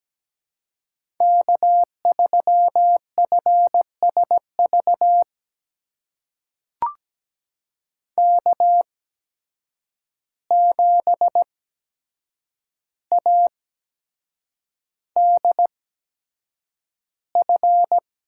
Morse code, heard as a keyed tone, about 17 wpm, spaced as for 7 wpm, 700 Hz. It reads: K3FSV K7ADF
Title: USA Call Signs - 5 Characters - 8x Character Spacing - 17wpm